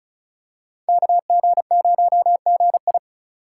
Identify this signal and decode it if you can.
Morse code, heard as a keyed tone, about 36 wpm, 700 Hz. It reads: KG0GI
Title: USA Call Signs - 5 Characters - 3x Character Spacing - 35wpm